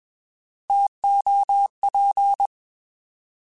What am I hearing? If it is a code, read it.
Morse code, heard as a keyed tone, about 21 wpm, 800 Hz. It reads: TOP